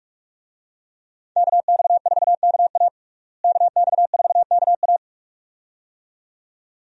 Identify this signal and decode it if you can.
Morse code, heard as a keyed tone, about 46 wpm, 700 Hz. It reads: KX4KA KX4KA
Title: USA Call Signs - 5 Characters - Repeated 2x - 45wpm